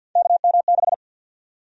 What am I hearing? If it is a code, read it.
Morse code, heard as a keyed tone, about 49 wpm, 700 Hz. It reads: KM6
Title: USA Call Sign Prefixes - Repeated 2x - 50wpm